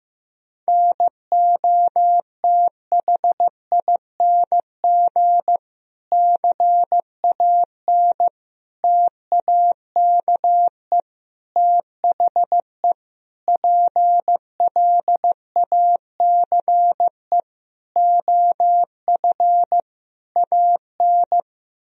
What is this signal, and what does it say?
Morse code, heard as a keyed tone, about 15 wpm, 700 Hz. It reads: NOTHING CAN TAKE THE PLACE OF AN
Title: Sentences from Top 1000 Words - Review Part 4 - 15wpm